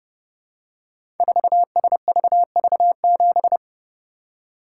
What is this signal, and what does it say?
Morse code, heard as a keyed tone, about 30 wpm, 700 Hz. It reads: 4SVV7
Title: Code Groups - 5 Letter:Numbers - 30wpm